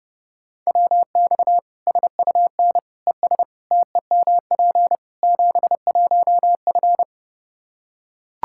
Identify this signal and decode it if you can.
Morse code, heard as a keyed tone, about 30 wpm, 700 Hz. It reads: WX SUN ES TEMP 71F
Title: Practice QSO - Exchange 2 - Info - 30wpm